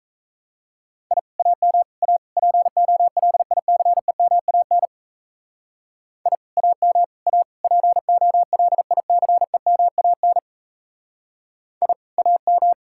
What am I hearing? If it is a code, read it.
Morse code, heard as a keyed tone, about 34 wpm, 700 Hz. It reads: I AM A POLICEMAN I AM A POLICEMAN I AM
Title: Speed-Racing Common QSO Phrases - 28wpm